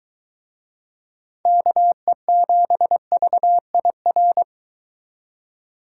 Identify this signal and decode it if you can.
Morse code, heard as a keyed tone, about 23 wpm, 700 Hz. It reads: KE7VIR